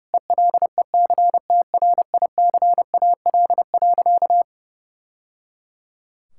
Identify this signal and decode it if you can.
Morse code, heard as a keyed tone, about 30 wpm, 700 Hz. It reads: ELECTRICAL.
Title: Sentences from Top 1000 Words - Review Part 6 - 30wpm